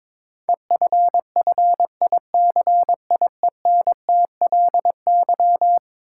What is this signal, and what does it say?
Morse code, heard as a keyed tone, about 21 wpm, 700 Hz. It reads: EFFICIENTLY
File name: practice / 22wpm / Eleven Letter Words - 22wpm